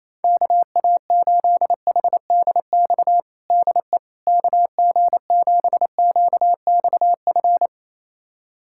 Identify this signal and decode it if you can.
Morse code, heard as a keyed tone, about 28 wpm, 700 Hz. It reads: KA8HDX DE KG7QXF
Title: USA Callsign DE Callsign - 6 Characters - 28wpm